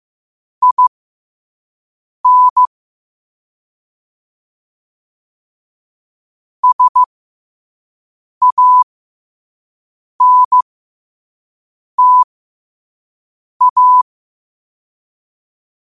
Morse code, heard as a keyed tone, about 15 wpm, spaced as for 5 wpm, 1000 Hz. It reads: IN SANTA
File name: qotdfiles / 5wpm